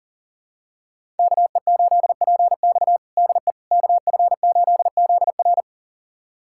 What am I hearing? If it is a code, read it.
Morse code, heard as a keyed tone, about 40 wpm, 700 Hz. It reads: KE8PX DE KF8ZR
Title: USA Callsign DE Callsign - 5 Characters - 40wpm